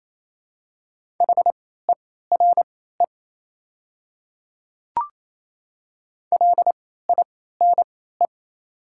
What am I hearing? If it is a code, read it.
Morse code, heard as a keyed tone, about 28 wpm, spaced as for 16 wpm, 700 Hz. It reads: HERE LINE
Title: 3x Character Spacing - Top 300 Words - 28wpm